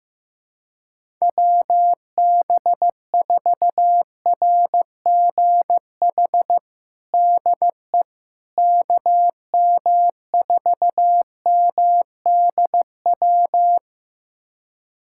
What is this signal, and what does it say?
Morse code, heard as a keyed tone, about 16 wpm, 700 Hz. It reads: WB4RGH DE KM4MDW
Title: USA Callsign DE Callsign 15wpm